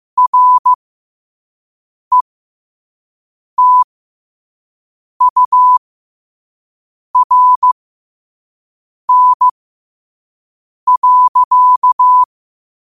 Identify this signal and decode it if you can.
Morse code, heard as a keyed tone, about 15 wpm, spaced as for 5 wpm, 1000 Hz. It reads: RETURN.
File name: qotdfiles / 5wpm